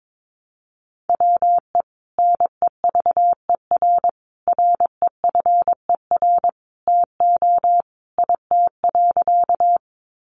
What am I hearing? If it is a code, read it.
Morse code, heard as a keyed tone, about 22 wpm, 700 Hz. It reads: WE NEVER REFER TO IT.
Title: Sentences from Top 1000 Words 22wpm